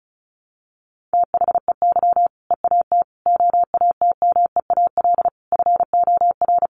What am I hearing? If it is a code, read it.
Morse code, heard as a keyed tone, about 34 wpm, 700 Hz. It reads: THEY EAT OATMEAL FOR
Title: Easy Phrases 35wpm